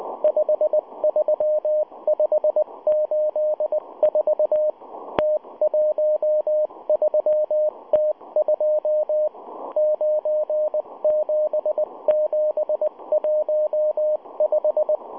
Morse code, heard as a keyed tone, about 19 wpm, 610 Hz, with radio noise. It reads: 53584 T13T2 97715